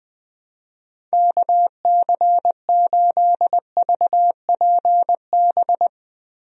Morse code, heard as a keyed tone, about 20 wpm, 700 Hz. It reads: KC8VPB
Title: USA Call Signs - 6 Characters - Repeated 2x - 20wpm